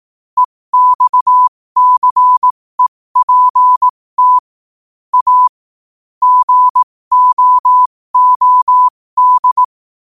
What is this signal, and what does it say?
Morse code, heard as a keyed tone, about 18 wpm, spaced as for 15 wpm, 1000 Hz. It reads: EXCEPT A GOOD
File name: qotdfiles / 15wpm